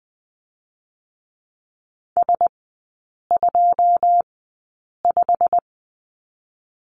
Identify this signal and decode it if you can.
Morse code, heard as a keyed tone, about 20 wpm, spaced as for 8 wpm, 700 Hz. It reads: S25
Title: 087 Lesson 15e - Number 5 - Three Characters 20wpm